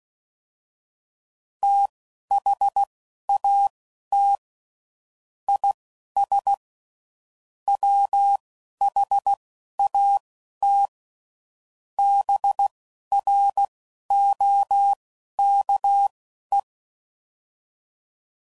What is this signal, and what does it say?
Morse code, heard as a keyed tone, about 16 wpm, spaced as for 11 wpm, 800 Hz. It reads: THAT IS WHAT BROKE